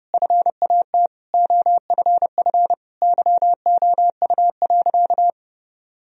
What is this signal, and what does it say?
Morse code, heard as a keyed tone, about 30 wpm, 700 Hz. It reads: FAT OFF YOU.
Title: Sentences from Top 1000 Words - Review Part 4 - 30wpm